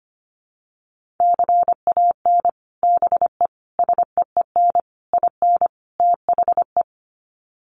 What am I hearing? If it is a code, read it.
Morse code, heard as a keyed tone, about 25 wpm, 700 Hz. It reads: CAN BE SEEN IN THE
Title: Sets of 5 Words Encore2 25wpm